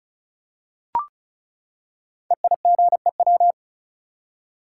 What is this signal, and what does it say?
Morse code, heard as a keyed tone, about 36 wpm, 700 Hz. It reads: EIGEW
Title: Code Groups - 5 Letters - 35wpm